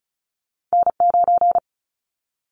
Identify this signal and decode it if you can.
Morse code, heard as a keyed tone, about 36 wpm, 700 Hz. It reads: N9
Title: Callsign Prefixes for Top 25 DXCC - 35wpm